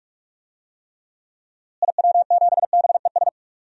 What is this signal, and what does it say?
Morse code, heard as a keyed tone, about 43 wpm, 700 Hz. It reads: IW7BES
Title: DXCC-100 Call Signs - 6 Characters - Repeated 3x - 45wpm